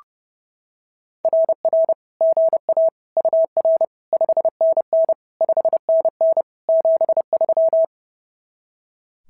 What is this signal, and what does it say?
Morse code, heard as a keyed tone, about 30 wpm, 650 Hz. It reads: RR GA UR 5NN 5NN 73
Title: SOTA QSO Practice 30wpm